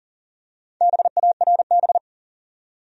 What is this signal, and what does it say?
Morse code, heard as a keyed tone, about 40 wpm, 700 Hz. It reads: BARB
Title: USA Common Names - Max 5 Letters - 40wpm